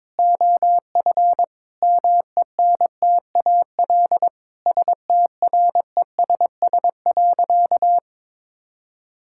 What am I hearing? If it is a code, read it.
Morse code, heard as a keyed tone, about 22 wpm, 700 Hz. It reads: OF MENTAL STRESS.
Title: Sentences from Top 1000 Words - Review Part 6 - 22wpm